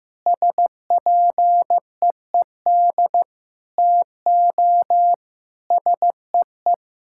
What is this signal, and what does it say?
Morse code, heard as a keyed tone, about 16 wpm, 700 Hz. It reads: SPEED TO SEE
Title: Sentences from Top 1000 Words - Review Part 5 - 15wpm